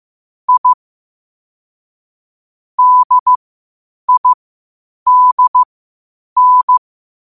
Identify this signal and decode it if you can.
Morse code, heard as a keyed tone, about 15 wpm, spaced as for 8 wpm, 1000 Hz. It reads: I DIDN
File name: qotdfiles / 8wpm